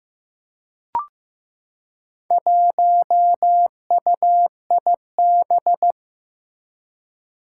Morse code, heard as a keyed tone, about 15 wpm, 700 Hz. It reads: 1UIB
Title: Four Challenging Letter-Number Combinations 15wpm